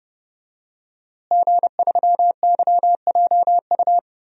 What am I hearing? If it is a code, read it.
Morse code, heard as a keyed tone, about 30 wpm, 700 Hz. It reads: G3YJU
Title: DXCC-100 Call Signs - 5 Characters - Repeated 3x - 30wpm